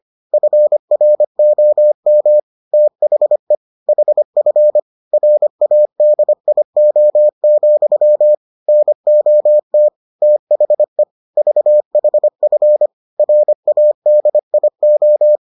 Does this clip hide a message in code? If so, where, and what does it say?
Yes: Morse code from 0.3 to 15.4 s, heard as a keyed tone, about 25 wpm, 600 Hz. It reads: FROM THE HF RADIO, NOT THE VHF RADIO